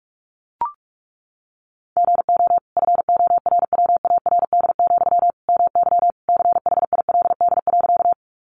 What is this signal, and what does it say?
Morse code, heard as a keyed tone, about 44 wpm, 700 Hz. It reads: GO FORWARD, MY CHILD.
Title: Sentences from Top 1300 Words 45wpm